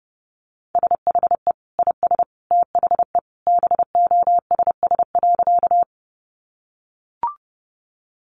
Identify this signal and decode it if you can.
Morse code, heard as a keyed tone, about 30 wpm, 700 Hz. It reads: SHE IS THE BOSS.
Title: Sentences from Top 700 Words 30wpm